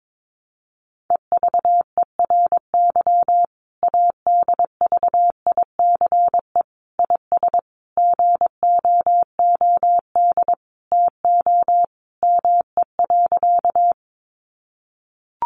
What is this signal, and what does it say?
Morse code, heard as a keyed tone, about 22 wpm, 700 Hz. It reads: EVERY ADVICE IS GOOD TO ME.
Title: Sentences from Top 1100 Words 22wpm